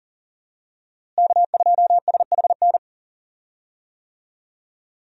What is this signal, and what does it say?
Morse code, heard as a keyed tone, about 40 wpm, 700 Hz. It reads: K2SHN
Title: USA Call Signs - 5 Characters - Repeated 2x - 40wpm